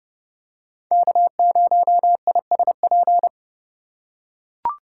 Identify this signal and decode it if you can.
Morse code, heard as a keyed tone, about 28 wpm, 700 Hz. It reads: K0ISP